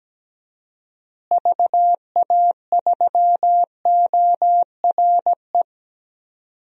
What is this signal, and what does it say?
Morse code, heard as a keyed tone, about 17 wpm, 700 Hz. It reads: VA3ORE
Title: DXCC-100 Call Signs - 6 Characters - Repeated 2x - 17wpm